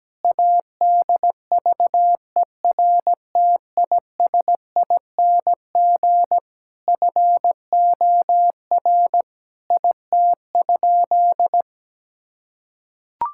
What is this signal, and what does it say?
Morse code, heard as a keyed tone, about 17 wpm, 700 Hz. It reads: ADVERTISING FOR IT?
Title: Sentences from Top 1000 Words - Review Part 6 - 17wpm